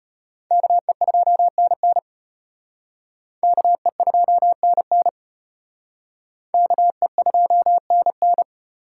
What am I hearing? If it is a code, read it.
Morse code, heard as a keyed tone, about 30 wpm, 700 Hz. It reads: KE2NN KE2NN KE2NN
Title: Speed Racing - USA Call Signs 25wpm